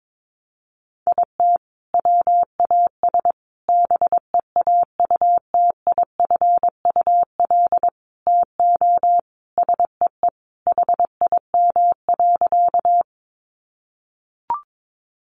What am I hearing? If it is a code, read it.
Morse code, heard as a keyed tone, about 22 wpm, 700 Hz. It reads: IT WAS BEAUTIFUL TO SEE HIM.
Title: Sentences from Top 900 Words 22wpm